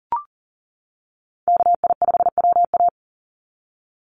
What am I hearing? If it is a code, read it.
Morse code, heard as a keyed tone, about 40 wpm, 700 Hz. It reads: KI5WA
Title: NA SOTA Chaser Callsigns 40wpm